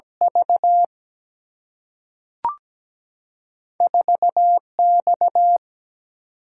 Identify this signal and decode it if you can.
Morse code, heard as a keyed tone, about 17 wpm, 700 Hz. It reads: V 4X
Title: Single-Letter Single-Number 17wpm